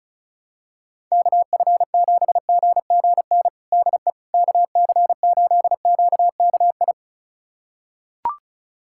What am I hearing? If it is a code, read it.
Morse code, heard as a keyed tone, about 34 wpm, 700 Hz. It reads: KF7GGN DE KC8QKI
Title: USA Callsign DE Callsign 35wpm